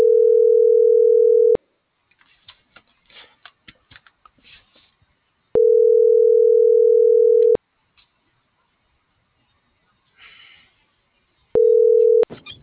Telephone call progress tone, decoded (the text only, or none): ringback tone